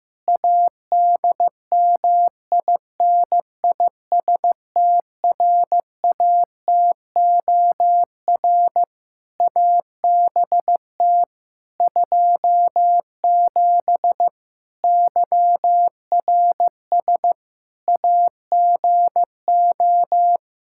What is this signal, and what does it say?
Morse code, heard as a keyed tone, about 15 wpm, 700 Hz. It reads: ADMINISTRATOR ABT 27 YRS AGO
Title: K1USN SST QSO Practice 15wpm